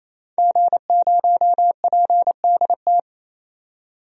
Morse code, heard as a keyed tone, about 28 wpm, 700 Hz. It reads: G0PDT
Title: DXCC-100 Call Signs - 5 Characters - 28wpm